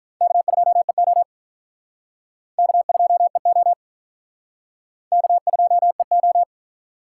Morse code, heard as a keyed tone, about 46 wpm, 700 Hz. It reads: K2EO K2EO K2EO